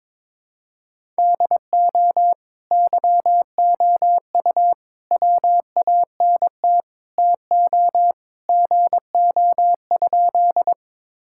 Morse code, heard as a keyed tone, about 21 wpm, 700 Hz. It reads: DO YOU WANT TO GO?
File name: practice / 22wpm / Sentences with 2-Word Phrases - 22wpm